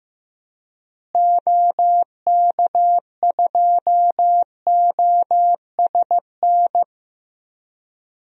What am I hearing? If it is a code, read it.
Morse code, heard as a keyed tone, about 15 wpm, 700 Hz. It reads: OK2OSN
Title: DXCC-100 Call Signs - 6 Characters - Repeated 3x - 15wpm